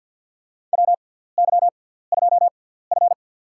Morse code, heard as a keyed tone, about 49 wpm, spaced as for 17 wpm, 700 Hz. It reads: WY2F
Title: USA Call Signs - 4 Characters - 6x Character Spacing - 50wpm